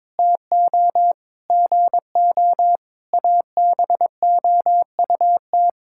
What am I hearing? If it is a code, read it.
Morse code, heard as a keyed tone, about 22 wpm, 700 Hz. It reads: TO GO ABOUT
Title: Sentences from Top 1600 Words 22wpm